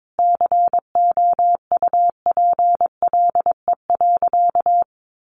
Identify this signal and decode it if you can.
Morse code, heard as a keyed tone, about 22 wpm, 700 Hz. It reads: COUPLE.
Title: Sentences from Top 1100 Words 22wpm